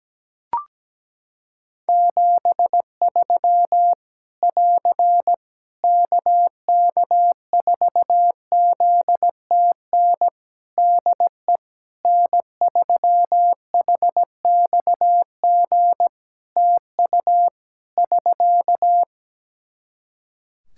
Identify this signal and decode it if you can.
Morse code, heard as a keyed tone, about 17 wpm, 700 Hz. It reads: 73 <AR> KK4ZTN DE N3HXG TU <SK>
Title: Practice QSO - Exchange 4 - Ending - 17wpm